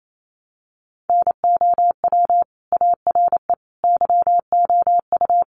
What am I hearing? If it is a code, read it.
Morse code, heard as a keyed tone, about 28 wpm, 690 Hz. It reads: NOW ARE YOU
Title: Sentences from Top 900 Words 28wpm